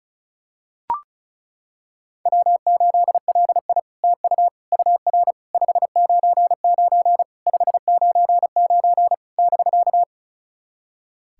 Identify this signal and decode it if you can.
Morse code, heard as a keyed tone, about 36 wpm, 700 Hz. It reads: W8LI TU UR 599 599 <BK>
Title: SOTA QSO Practice 35wpm